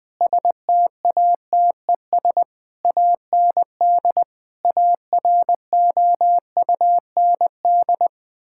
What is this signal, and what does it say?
Morse code, heard as a keyed tone, about 20 wpm, 700 Hz. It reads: STATES AND AROUND
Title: Sentences with 8-Word Phrases - 20wpm